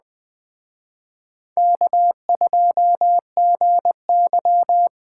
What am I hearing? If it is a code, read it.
Morse code, heard as a keyed tone, about 20 wpm, 700 Hz. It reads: K2GY